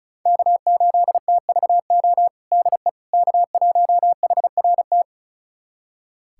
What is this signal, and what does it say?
Morse code, heard as a keyed tone, about 34 wpm, 700 Hz. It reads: K8TVO DE K1HRT
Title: USA Callsign DE Callsign - 5 Characters - 35wpm